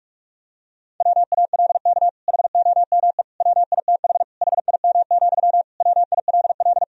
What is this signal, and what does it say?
Morse code, heard as a keyed tone, about 46 wpm, 700 Hz. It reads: WALK HOME WITH HIM, WILL
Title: Sentences from Top 1000 Words - Review Part 1 - 45wpm